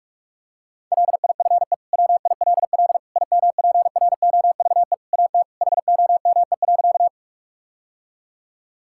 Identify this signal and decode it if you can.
Morse code, heard as a keyed tone, about 41 wpm, 700 Hz. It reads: LIFE WILL IMPROVE AT HOME.
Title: Sentences from Top 1000 Words - Review Part 5 - 45wpm